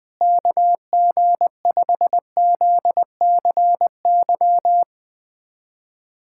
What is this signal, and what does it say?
Morse code, heard as a keyed tone, about 20 wpm, 700 Hz. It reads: KG5ZCY
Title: USA Callsign DE Callsign - 6 Characters - 20wpm